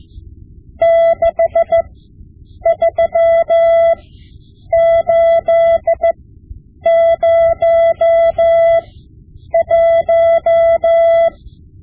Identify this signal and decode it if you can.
Morse code, heard as a keyed tone, about 15 wpm, spaced as for 8 wpm, 670 Hz. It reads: 63801
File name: Morse Lessons / Morse Lessons / Tape-5B